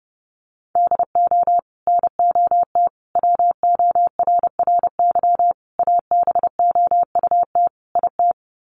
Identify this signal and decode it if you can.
Morse code, heard as a keyed tone, about 28 wpm, 700 Hz. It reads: DO NOT WORRY ABOUT IT
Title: Sets of 5 Words 30wpm